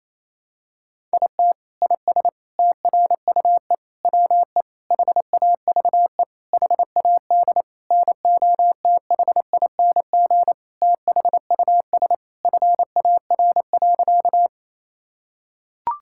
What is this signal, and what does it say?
Morse code, heard as a keyed tone, about 28 wpm, 700 Hz. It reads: IT IS TRUE WE HAVE HAD NOTHING THUS FAR.
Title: Sentences from Top 1000 Words - Review Part 5 - 28wpm